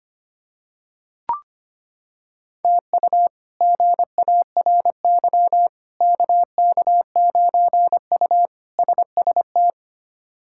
Morse code, heard as a keyed tone, about 25 wpm, 700 Hz. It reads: TU GARY KK9U SST
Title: K1USN SST QSO Practice - 25wpm